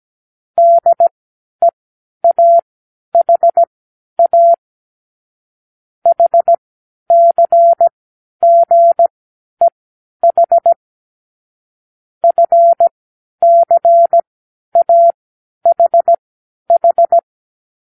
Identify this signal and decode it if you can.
Morse code, heard as a keyed tone, about 17 wpm, spaced as for 10 wpm, 680 Hz. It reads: DEAHA HCGEH FCAHH
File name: k7qo / Lesson12(A-H)